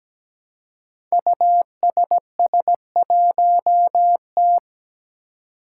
Morse code, heard as a keyed tone, about 17 wpm, 700 Hz. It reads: USS1T
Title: Five Challenging Letter-Number Combinations 17wpm